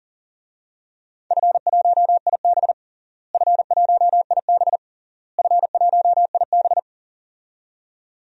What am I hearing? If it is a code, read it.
Morse code, heard as a keyed tone, about 40 wpm, 700 Hz. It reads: F1IB F1IB F1IB
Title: DXCC-100 Call Signs - 4 Characters - Repeated 3x - 40wpm